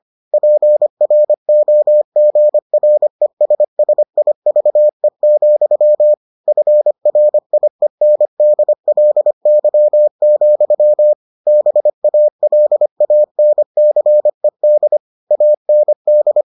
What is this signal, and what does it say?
Morse code, heard as a keyed tone, about 25 wpm, 600 Hz. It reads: PROGRESSIVE, FRIENDLY, BALANCED AND